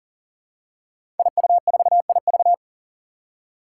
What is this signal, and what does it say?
Morse code, heard as a keyed tone, about 40 wpm, 700 Hz. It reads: IU4IV
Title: Five Challenging Letter-Number Combinations 40wpm